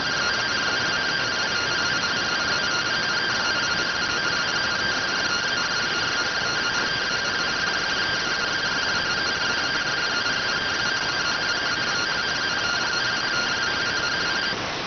Two-tone teleprinter signal, about 170 Hz shift.